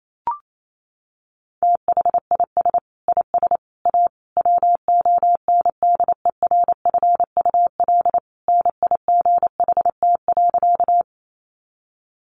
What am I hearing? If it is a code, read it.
Morse code, heard as a keyed tone, about 28 wpm, 700 Hz. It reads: THIS IS A WONDERFUL NIGHT.
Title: Sentences from Top 1100 Words 28wpm